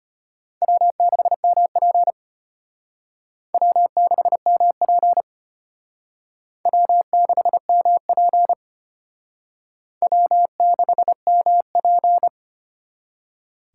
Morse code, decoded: W6MP W6MP W6MP W6MP

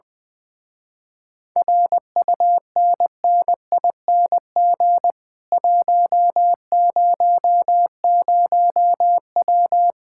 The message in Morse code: RUNNING 100W

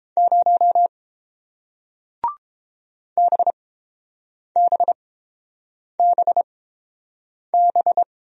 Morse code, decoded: 0 BBBB

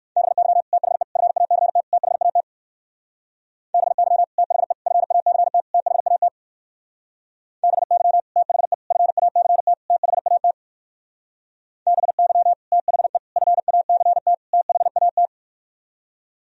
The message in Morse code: BY THE FACT THAT BY THE FACT THAT BY THE FACT THAT BY THE FACT THAT

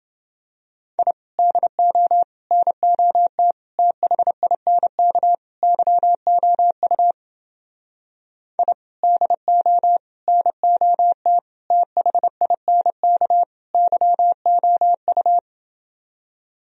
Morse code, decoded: I DO NOT THINK YOU I DO NOT THINK YOU

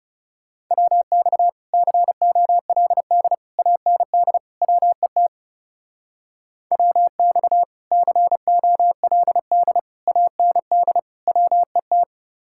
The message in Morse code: WX COLD AND WET WX COLD AND WET

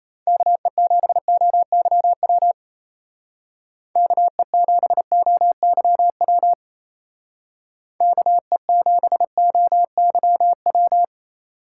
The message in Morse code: KE7OYW KE7OYW KE7OYW

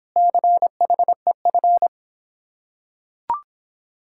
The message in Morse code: CHEF

